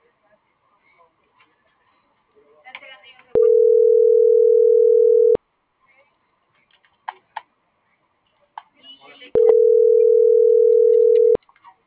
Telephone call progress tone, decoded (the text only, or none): ringback tone